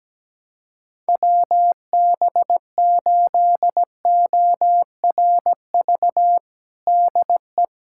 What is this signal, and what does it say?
Morse code, heard as a keyed tone, about 17 wpm, 700 Hz. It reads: WB8ORV DE
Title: USA Callsign DE Callsign - 6 Characters - 17wpm